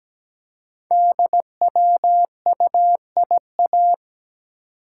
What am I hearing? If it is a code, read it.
Morse code, heard as a keyed tone, about 17 wpm, 700 Hz. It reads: DWUIA
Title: Code Groups - 5 Letter:Numbers - 17wpm